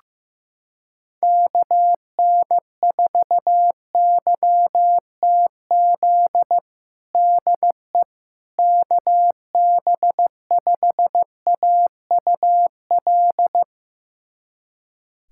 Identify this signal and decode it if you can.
Morse code, heard as a keyed tone, about 15 wpm, 700 Hz. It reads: KN4YTZ DE KB5AUL
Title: USA Callsign DE Callsign - 6 Characters - 15wpm